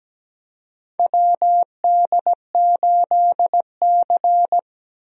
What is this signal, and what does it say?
Morse code, heard as a keyed tone, about 17 wpm, 700 Hz. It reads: WD8C
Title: USA Call Signs - 4 Characters - Repeated 2x - 17wpm